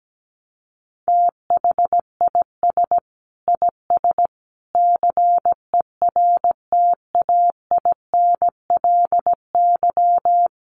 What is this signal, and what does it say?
Morse code, heard as a keyed tone, about 17 wpm, 700 Hz. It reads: THIS IS CERTAINLY
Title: Sentences from Top 1300 Words 17wpm